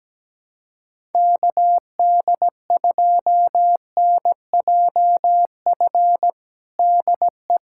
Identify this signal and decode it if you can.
Morse code, heard as a keyed tone, about 17 wpm, 700 Hz. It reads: KD2NJF DE
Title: USA Callsign DE Callsign - 6 Characters - 17wpm